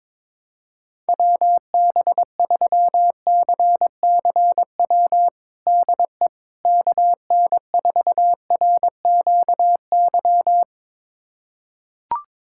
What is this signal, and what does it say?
Morse code, heard as a keyed tone, about 22 wpm, 700 Hz. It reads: WB3CCW DE KN4RQY
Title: USA Callsign DE Callsign - 6 Characters - 22wpm